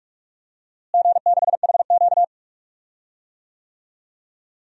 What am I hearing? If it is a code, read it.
Morse code, heard as a keyed tone, about 46 wpm, 700 Hz. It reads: G6HQ
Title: DXCC-100 Call Signs - 4 Characters - Repeated 2x - 45wpm